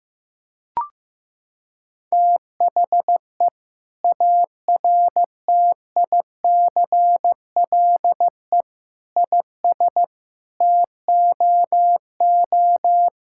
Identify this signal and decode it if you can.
Morse code, heard as a keyed tone, about 15 wpm, 700 Hz. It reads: THE ARTICLE IS TOO
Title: Sentences from Top 1000 Words - Review Part 5 - 15wpm